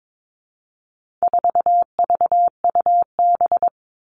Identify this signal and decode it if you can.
Morse code, heard as a keyed tone, about 22 wpm, 700 Hz. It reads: 4VUB
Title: All Cut Numbers - 4-Digit 22wpm